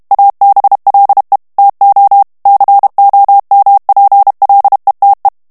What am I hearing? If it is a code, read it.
Morse code, heard as a keyed tone, about 32 wpm, 800 Hz. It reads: ABLE TO COMPLETE
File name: ANGLAIS texte clair 3 vitesse 30 mots minute